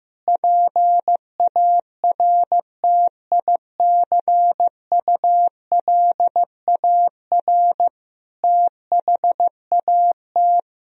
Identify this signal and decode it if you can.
Morse code, heard as a keyed tone, about 15 wpm, 700 Hz. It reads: PARTICULAR THAT
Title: Sentences from Top 1000 Words - Review Part 4 - 15wpm